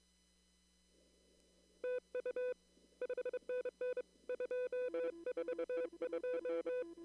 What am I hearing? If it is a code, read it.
Morse code, heard as a keyed tone, about 23 wpm, 500 Hz. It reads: TU 5NN 242
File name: CW / kt5j wpx cw-op-k5pi